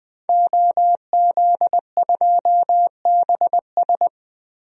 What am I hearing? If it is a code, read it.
Morse code, heard as a keyed tone, about 21 wpm, 700 Hz. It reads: OZ2BS
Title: DXCC-100 Call Signs - 5 Characters - Repeated 2x - 20wpm